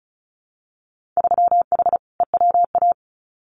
Morse code, heard as a keyed tone, about 36 wpm, 700 Hz. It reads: 3H EWA
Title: Winter Field Day Exchanges Encore 35wpm